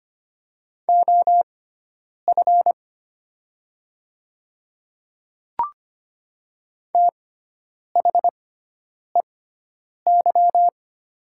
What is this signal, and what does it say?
Morse code, heard as a keyed tone, about 25 wpm, spaced as for 9 wpm, 700 Hz. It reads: OF THEY
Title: 6x Character Spacing - Top 100 Words - 25wpm